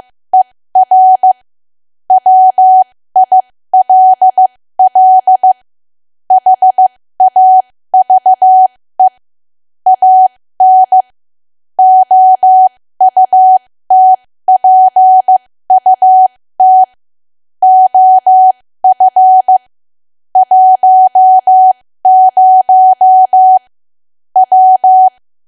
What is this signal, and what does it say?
Morse code, heard as a keyed tone, about 15 wpm, spaced as for 13 wpm, 750 Hz. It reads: ER WILL HAVE AN OUTPUT OF 10 W
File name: cwa / W1AW-Level2 / 13WPM